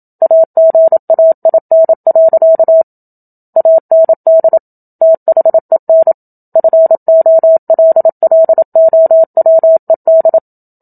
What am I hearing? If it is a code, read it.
Morse code, heard as a keyed tone, about 27 wpm, 650 Hz. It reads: AGAIN. AND THEN FOLLOWED
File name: Book Two / 01-Chapter 1 - 27 WPM